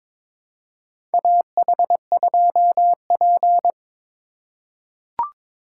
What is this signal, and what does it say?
Morse code, heard as a keyed tone, about 22 wpm, 700 Hz. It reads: AH2P
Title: USA Call Signs - 4 Characters - 2x Character Spacing - 22wpm